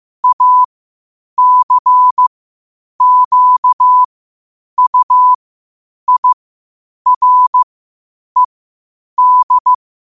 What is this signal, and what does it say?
Morse code, heard as a keyed tone, about 15 wpm, spaced as for 8 wpm, 1000 Hz. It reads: ACQUIRED
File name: qotdfiles / 8wpm